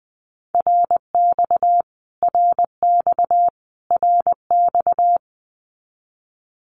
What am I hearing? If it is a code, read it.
Morse code, heard as a keyed tone, about 20 wpm, 700 Hz. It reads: RX RX RX
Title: Abbreviations Repeated 3x - 20wpm